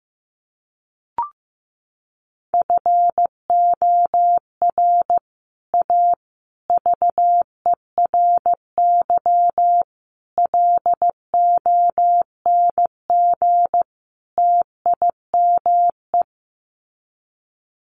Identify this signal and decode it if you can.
Morse code, heard as a keyed tone, about 15 wpm, 700 Hz. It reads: FOR A VERY LONG TIME
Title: Sets of 5 Words 15wpm